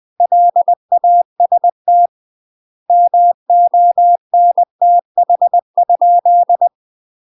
Morse code, heard as a keyed tone, about 20 wpm, 700 Hz. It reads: LAST MONTH ?
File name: UK MASTER QSOs TRAINING / 20WPM / UK MASTER QSOs 0040